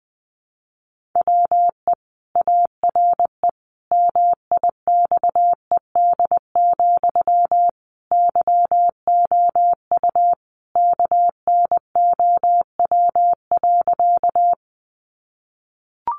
Morse code, heard as a keyed tone, about 19 wpm, 700 Hz. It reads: WE ARE MIXED, YOU KNOW.
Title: Sentences from Top 1500 Words 20wpm